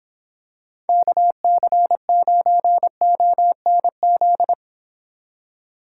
Morse code, heard as a keyed tone, about 26 wpm, 700 Hz. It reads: KC9ONZ